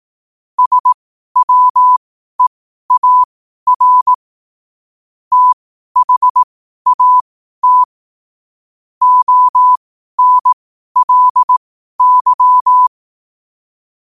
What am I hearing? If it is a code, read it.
Morse code, heard as a keyed tone, about 18 wpm, spaced as for 12 wpm, 1000 Hz. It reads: SWEAR THAT ONLY
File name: qotdfiles / 12wpm